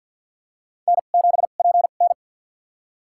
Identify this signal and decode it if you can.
Morse code, heard as a keyed tone, about 53 wpm, spaced as for 37 wpm, 700 Hz. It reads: N7PN
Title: USA Call Signs - 4 Characters - 2x Character Spacing - 50wpm